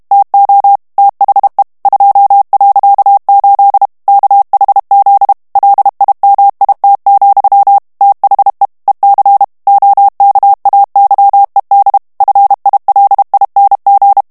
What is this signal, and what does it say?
Morse code, heard as a keyed tone, about 32 wpm, 800 Hz. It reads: TO THE 2.8 KHZ LIMIT, THE EC OKAYED FILING
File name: morse mp3 / ANGLAIS texte clair 3 vitesse 30 mots minute